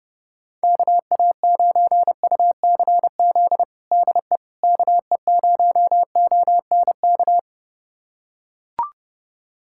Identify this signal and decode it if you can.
Morse code, heard as a keyed tone, about 30 wpm, 700 Hz. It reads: KA9UCZ DE KE0ONK
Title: USA Callsign DE Callsign - 6 Characters - 30wpm